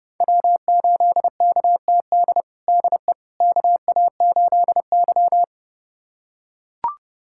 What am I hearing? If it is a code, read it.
Morse code, heard as a keyed tone, about 32 wpm, 700 Hz. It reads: W8KTD DE KA8Y
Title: USA Callsign DE Callsign 30wpm